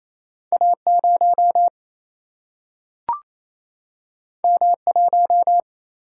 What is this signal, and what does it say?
Morse code, heard as a keyed tone, about 28 wpm, 700 Hz. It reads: A0 M1
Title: Single-Letter Single-Number 28wpm